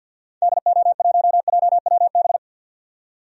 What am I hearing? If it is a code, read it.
Morse code, heard as a keyed tone, about 52 wpm, 700 Hz. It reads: DO1JWB
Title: DXCC-100 Call Signs - 6 Characters - 50wpm